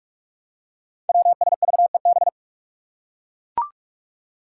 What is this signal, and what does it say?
Morse code, heard as a keyed tone, about 44 wpm, 700 Hz. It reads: WSVEB
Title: Five Challenging Letter-Number Combinations 45wpm